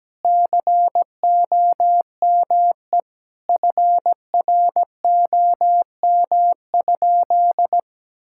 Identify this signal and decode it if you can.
Morse code, heard as a keyed tone, about 17 wpm, 700 Hz. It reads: COME FROM?
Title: Sentences from Top 1000 Words - Review Part 6 - 17wpm